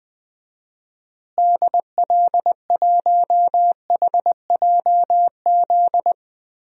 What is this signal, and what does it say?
Morse code, heard as a keyed tone, about 20 wpm, 700 Hz. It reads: DL1HJZ
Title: DXCC-100 Call Signs - 6 Characters - Repeated 3x - 20wpm